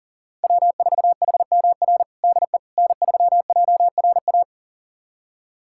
Morse code, heard as a keyed tone, about 40 wpm, 700 Hz. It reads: W4HMR DE N3JRA
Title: USA Callsign DE Callsign - 5 Characters - 40wpm